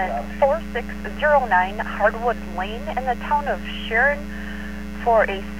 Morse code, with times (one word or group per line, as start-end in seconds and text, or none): none